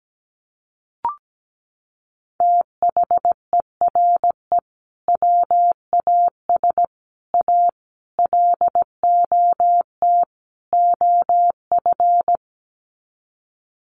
Morse code, heard as a keyed tone, about 17 wpm, 700 Hz. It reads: THERE WAS A LOT OF
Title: Sets of 5 Words 17wpm